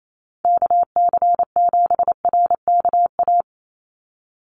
Morse code, heard as a keyed tone, about 28 wpm, 700 Hz. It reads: KC7RKA
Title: USA Call Signs 28wpm